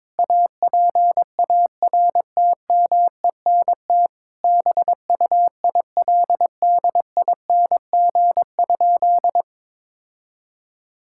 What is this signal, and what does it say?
Morse code, heard as a keyed tone, about 22 wpm, 700 Hz. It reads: APARTMENT BUILDING?
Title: Sentences from Top 1000 Words - Review Part 6 - 22wpm